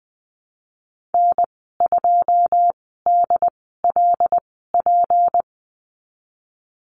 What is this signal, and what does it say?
Morse code, heard as a keyed tone, about 20 wpm, spaced as for 15 wpm, 700 Hz. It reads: N2DLP
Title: 173 Lesson 28g - Number 9 - Call Signs 20wpm